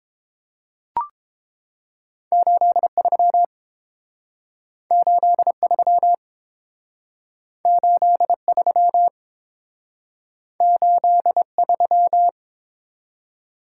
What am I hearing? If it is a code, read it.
Morse code, heard as a keyed tone, about 30 wpm, 700 Hz. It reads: 83 83 83 83